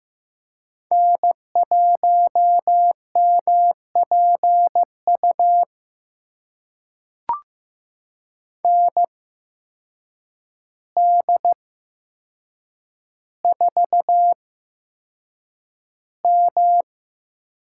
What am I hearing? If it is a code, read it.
Morse code, heard as a keyed tone, about 15 wpm, spaced as for 7 wpm, 700 Hz. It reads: N1MPU ND4M
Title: USA Call Signs - 8x Character Spacing - 15wpm